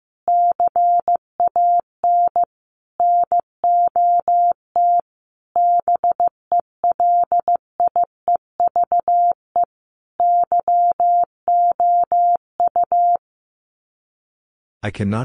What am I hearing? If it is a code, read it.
Morse code, heard as a keyed tone, about 15 wpm, 700 Hz. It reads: CAN NOT BELIEVE YOU
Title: Sets of 5 Words Encore 15wpm